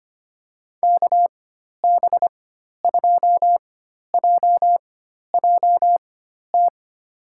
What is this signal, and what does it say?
Morse code, heard as a keyed tone, about 25 wpm, spaced as for 12 wpm, 700 Hz. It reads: KB2JJT
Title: USA Call Signs - 6 Characters - 4x Character Spacing - 25wpm